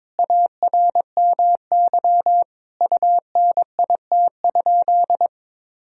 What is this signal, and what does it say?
Morse code, heard as a keyed tone, about 21 wpm, 700 Hz. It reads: ARMY UNIT?
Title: Sentences from Top 1000 Words - Review Part 2 - 22wpm